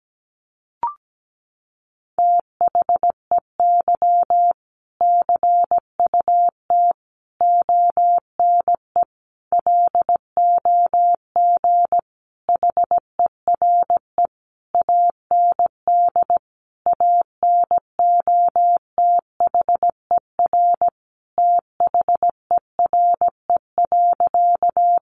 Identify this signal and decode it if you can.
Morse code, heard as a keyed tone, about 17 wpm, 700 Hz. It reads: THEY CUT ONE LOG HERE AND ANOTHER THERE.
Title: Sentences from Top 1400 Words 17wpm